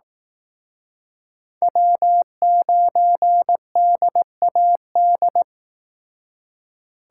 Morse code, heard as a keyed tone, about 18 wpm, 700 Hz. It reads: W9DAD
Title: Speed Racing - USA 1x3 Call Signs 15wpm